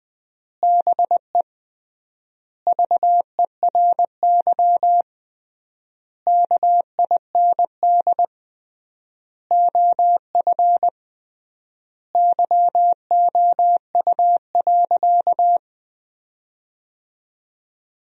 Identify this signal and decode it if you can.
Morse code, heard as a keyed tone, about 20 wpm, 700 Hz. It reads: BE VERY KIND OF YOU.
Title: 3x Word-Spacing - Sentences from Top 200 Words - 20wpm